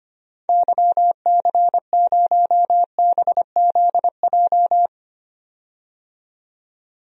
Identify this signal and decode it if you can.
Morse code, heard as a keyed tone, about 25 wpm, 700 Hz. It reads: YC0BZJ
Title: DXCC-100 Call Signs - 6 Characters - 25wpm